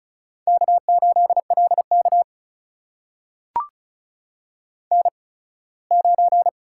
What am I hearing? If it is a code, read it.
Morse code, heard as a keyed tone, about 36 wpm, spaced as for 21 wpm, 700 Hz. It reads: K8LK N9
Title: USA Call Signs - 4 Characters - 8x Character Spacing - 35wpm